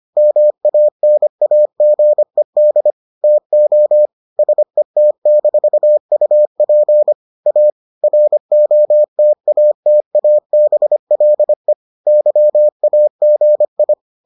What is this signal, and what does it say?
Morse code, heard as a keyed tone, about 25 wpm, 600 Hz. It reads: MANAGED TO SET-UP A ROTATABLE YAGI